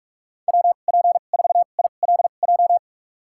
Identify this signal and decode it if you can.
Morse code, heard as a keyed tone, about 46 wpm, spaced as for 32 wpm, 700 Hz. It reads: WP4ILJ
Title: USA Call Signs - 6 Characters - 2x Character Spacing - 45wpm